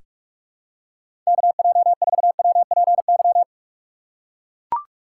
Morse code, heard as a keyed tone, about 42 wpm, 700 Hz. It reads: KJ4WPY
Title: USA Call Signs - 6 Characters - 3x Character Spacing - 45wpm